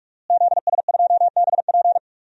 Morse code, heard as a keyed tone, about 46 wpm, 700 Hz. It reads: ZS2BP
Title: DXCC-100 Call Signs - 5 Characters - Repeated 2x - 45wpm